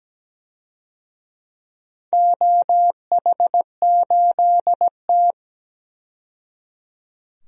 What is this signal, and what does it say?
Morse code, heard as a keyed tone, about 17 wpm, 700 Hz. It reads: OH8T
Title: DXCC-100 Call Signs - 4 Characters - 17wpm